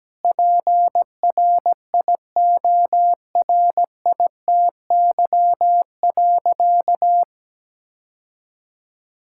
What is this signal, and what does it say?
Morse code, heard as a keyed tone, about 17 wpm, 700 Hz. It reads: PRIORITY.
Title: Sentences from Top 1600 Words 17wpm